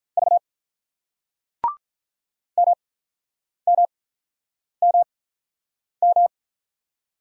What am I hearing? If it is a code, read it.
Morse code, heard as a keyed tone, about 46 wpm, spaced as for 9 wpm, 700 Hz. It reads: V MMMM